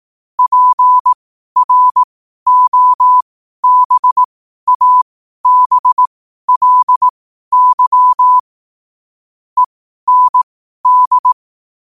Morse code, heard as a keyed tone, about 18 wpm, spaced as for 12 wpm, 1000 Hz. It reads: PROBABLY END
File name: qotdfiles / 12wpm